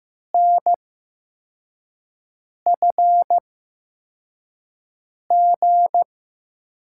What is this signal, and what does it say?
Morse code, heard as a keyed tone, about 15 wpm, spaced as for 4 wpm, 700 Hz. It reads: NFG
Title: USA Call Signs - 5 Characters - 8x Character Spacing - 15wpm